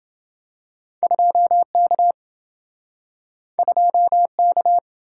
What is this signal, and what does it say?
Morse code, heard as a keyed tone, about 27 wpm, 700 Hz. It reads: 2K 2K